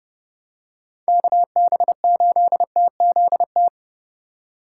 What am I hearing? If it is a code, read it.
Morse code, heard as a keyed tone, about 32 wpm, 700 Hz. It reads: KB8TZT
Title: USA Call Signs - 6 Characters - 4x Character Spacing - 30wpm